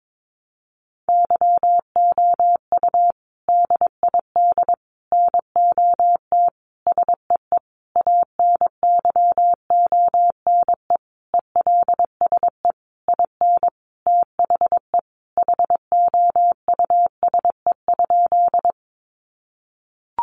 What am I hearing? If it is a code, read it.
Morse code, heard as a keyed tone, about 22 wpm, 700 Hz. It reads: YOU DID NOT SEE ANYONE ELSE IN THE HOUSE?
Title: Sentences from Top 900 Words 22wpm